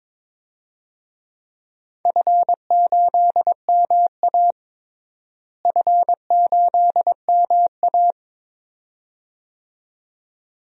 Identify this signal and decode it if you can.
Morse code, heard as a keyed tone, about 22 wpm, 700 Hz. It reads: F8MA F8MA
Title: DXCC-100 Call Signs - 4 Characters - Repeated 2x - 22wpm